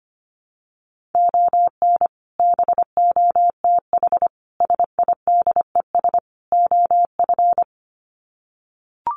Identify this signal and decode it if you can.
Morse code, heard as a keyed tone, about 25 wpm, 700 Hz. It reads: ON BOTH SIDES OF
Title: Sets of 4 Words 25wpm